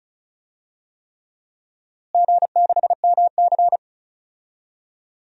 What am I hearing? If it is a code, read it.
Morse code, heard as a keyed tone, about 34 wpm, 700 Hz. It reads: G6MC
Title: DXCC-100 Call Signs - 4 Characters - Repeated 3x - 35wpm